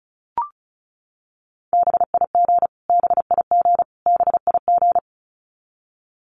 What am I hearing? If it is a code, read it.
Morse code, heard as a keyed tone, about 34 wpm, 700 Hz. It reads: BIG BIG BIG
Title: Top 500 Words Repeated 35wpm